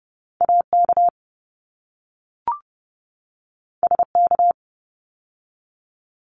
Morse code, heard as a keyed tone, about 30 wpm, 700 Hz. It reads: AK SK
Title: ARRL Sections 30wpm